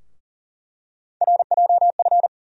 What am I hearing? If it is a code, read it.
Morse code, heard as a keyed tone, about 40 wpm, 700 Hz. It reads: RJF
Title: USA Call Sign Postfixes - 3 Letters 40wpm